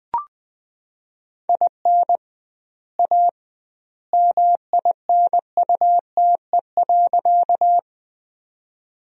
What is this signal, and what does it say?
Morse code, heard as a keyed tone, about 20 wpm, 700 Hz. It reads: IN A MINUTE.
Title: 2x Word-Spacing - Sentences from Top 500 Words - 20wpm